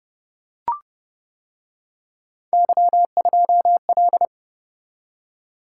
Morse code, heard as a keyed tone, about 29 wpm, 700 Hz. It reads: Y2L